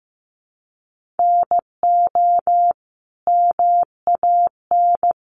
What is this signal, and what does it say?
Morse code, heard as a keyed tone, about 15 wpm, 700 Hz. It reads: NO MAN
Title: Sentences from Top 1400 Words 15wpm